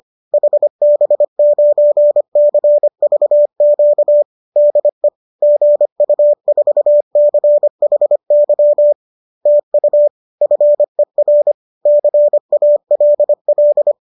Morse code, heard as a keyed tone, about 25 wpm, 590 Hz. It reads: HB9CVQ DE GU4CHY TU FER CALL
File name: qso210 25